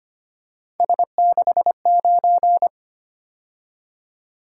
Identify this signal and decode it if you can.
Morse code, heard as a keyed tone, about 25 wpm, 700 Hz. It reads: S69
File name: Three Letter-Number Combinations - 25wpm